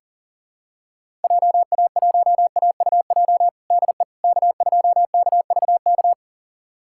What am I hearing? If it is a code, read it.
Morse code, heard as a keyed tone, about 40 wpm, 700 Hz. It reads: JA1AUJ DE K2KVK
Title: USA Callsign DE Callsign 40wpm